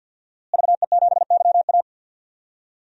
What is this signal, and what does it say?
Morse code, heard as a keyed tone, about 47 wpm, 700 Hz. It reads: VE7YA
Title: DXCC-100 Call Signs - 5 Characters - Repeated 2x - 50wpm